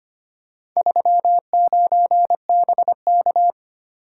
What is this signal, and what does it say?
Morse code, heard as a keyed tone, about 25 wpm, 700 Hz. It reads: 39BK
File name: practice / 25wpm / Four Letter-Number Combinations - 25wpm